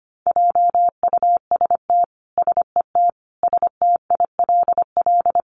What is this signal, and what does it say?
Morse code, heard as a keyed tone, about 25 wpm, 700 Hz. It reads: JUST SET STILL
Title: Sentences from Top 300 Words 25wpm